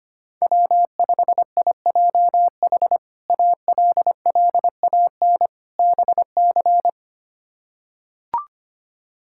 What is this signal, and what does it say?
Morse code, heard as a keyed tone, about 25 wpm, 700 Hz. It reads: W5IJH ALLAN BC
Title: K1USN SST QSO Practice - 25wpm